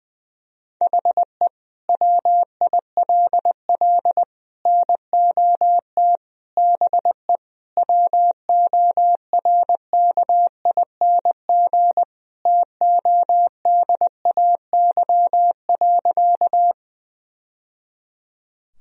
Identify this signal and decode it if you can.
Morse code, heard as a keyed tone, about 20 wpm, 700 Hz. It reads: HE WILL NOT BE WORKING TODAY.
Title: Sentences from Top 1000 Words - Review Part 4 - 20wpm